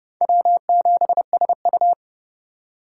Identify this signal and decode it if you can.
Morse code, heard as a keyed tone, about 27 wpm, 700 Hz. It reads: W7SU